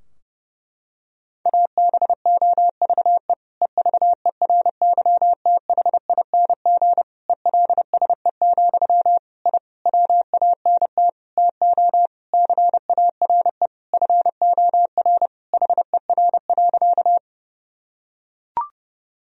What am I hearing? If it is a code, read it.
Morse code, heard as a keyed tone, about 30 wpm, 700 Hz. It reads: ABOVE EVERYTHING ELSE, I WANT TO CARE FOR HER.
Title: Sentences from Top 1000 Words - Review Part 2 - 30wpm